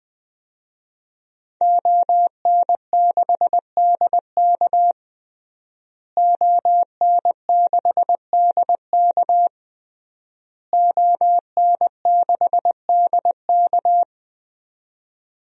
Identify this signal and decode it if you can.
Morse code, heard as a keyed tone, about 20 wpm, 700 Hz. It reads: ON6DK ON6DK ON6DK
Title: DXCC-100 Call Signs - 5 Characters - Repeated 3x - 20wpm